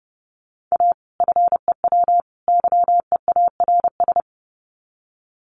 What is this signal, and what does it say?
Morse code, heard as a keyed tone, about 30 wpm, 700 Hz. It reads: A FEW YEARS
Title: Sets of 3 Words 30wpm